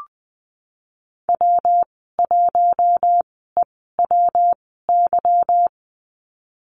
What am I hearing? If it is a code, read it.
Morse code, heard as a keyed tone, about 19 wpm, spaced as for 14 wpm, 700 Hz. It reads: W1EWY
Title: 186 Lesson 30g - Letter b - Call Signs 20wpm